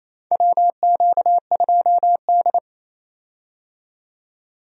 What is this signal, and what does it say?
Morse code, heard as a keyed tone, about 28 wpm, 700 Hz. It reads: WQ2D
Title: USA Call Signs - 4 Characters - 28wpm